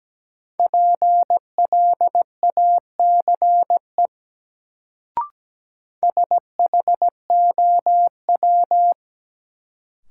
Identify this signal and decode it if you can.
Morse code, heard as a keyed tone, about 17 wpm, 700 Hz. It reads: PLACE SHOW
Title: Sentences from Top 200 Words - New Word Drill - 17wpm